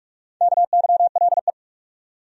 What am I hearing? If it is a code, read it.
Morse code, heard as a keyed tone, about 46 wpm, 700 Hz. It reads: KYLE
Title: USA Common Names - Max 4 Letters - 45wpm